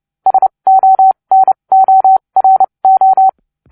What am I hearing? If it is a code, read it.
Morse code, heard as a keyed tone, about 29 wpm, spaced as for 24 wpm, 760 Hz. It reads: SONORO